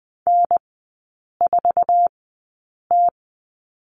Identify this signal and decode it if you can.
Morse code, heard as a keyed tone, about 20 wpm, spaced as for 8 wpm, 700 Hz. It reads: N4T
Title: 067 Lesson 12e - Letter d - Three Characters 20wpm